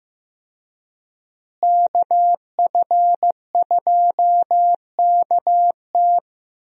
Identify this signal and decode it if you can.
Morse code, heard as a keyed tone, about 16 wpm, 700 Hz. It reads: KF2KT
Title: USA Call Signs - 5 Characters - Repeated 2x - 15wpm